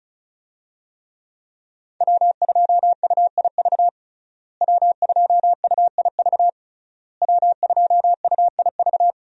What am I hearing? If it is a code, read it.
Morse code, heard as a keyed tone, about 36 wpm, 700 Hz. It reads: W2UIV W2UIV W2UIV
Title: USA Call Signs - 5 Characters - Repeated 3x - 35wpm